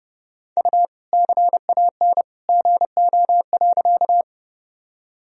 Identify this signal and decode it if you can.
Morse code, heard as a keyed tone, about 28 wpm, 700 Hz. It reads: U CAN GO.
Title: Sentences from Top 100 Words and Abbreviations 30wpm